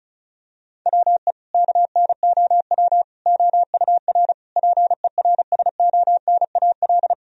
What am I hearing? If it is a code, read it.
Morse code, heard as a keyed tone, about 36 wpm, 700 Hz. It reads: WE KNOW OUR PERSONAL
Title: Sentences from Top 1000 Words - Review Part 4 - 35wpm